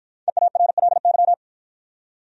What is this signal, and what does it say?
Morse code, heard as a keyed tone, about 53 wpm, 700 Hz. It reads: EARLY